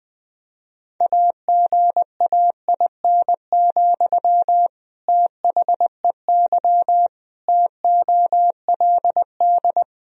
Morse code, heard as a keyed tone, about 20 wpm, 700 Hz. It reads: AGAIN, THEY TOLD
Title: Sentences from Top 1000 Words - Review Part 3 - 20wpm